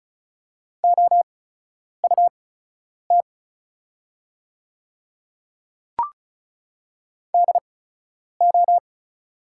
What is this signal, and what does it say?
Morse code, heard as a keyed tone, about 36 wpm, spaced as for 10 wpm, 700 Hz. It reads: OUT DO
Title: 8x Character Spacing - Top 100 Words - 35wpm